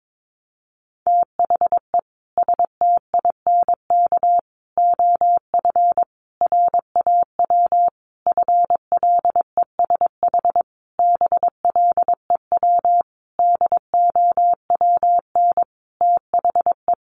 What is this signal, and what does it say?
Morse code, heard as a keyed tone, about 22 wpm, 700 Hz. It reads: THE STINK OF RAW FLESH BLEW DOWN THE
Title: A Sound Of Thunder 22wpm